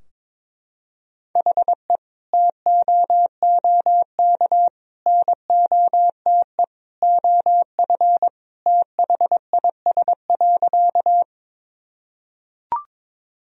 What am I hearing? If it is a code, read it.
Morse code, heard as a keyed tone, about 22 wpm, 700 Hz. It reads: HE TOOK NOTE OF THIS.
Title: Sentences from Top 1000 Words - Review Part 2 - 22wpm